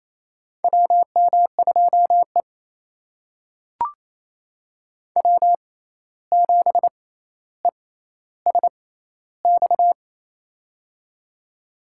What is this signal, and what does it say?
Morse code, heard as a keyed tone, about 28 wpm, spaced as for 13 wpm, 700 Hz. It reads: WM2E W7ESX
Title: USA Call Signs - 6x Character Spacing - 28wpm